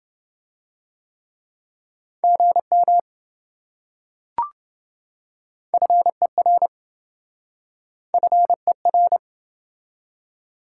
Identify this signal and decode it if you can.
Morse code, heard as a keyed tone, about 26 wpm, 700 Hz. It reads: GM FER FER